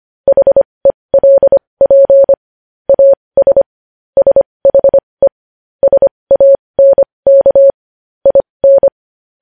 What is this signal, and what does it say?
Morse code, heard as a keyed tone, about 25 wpm, spaced as for 20 wpm, 550 Hz. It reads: HELP AS SHE SANK IN
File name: podcast / News headlines-1x-25x20